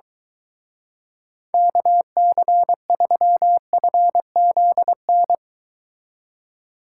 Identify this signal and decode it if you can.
Morse code, heard as a keyed tone, about 22 wpm, 700 Hz. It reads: KC3FZN